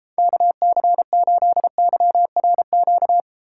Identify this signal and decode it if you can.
Morse code, heard as a keyed tone, about 33 wpm, 700 Hz. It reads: KC8YRQ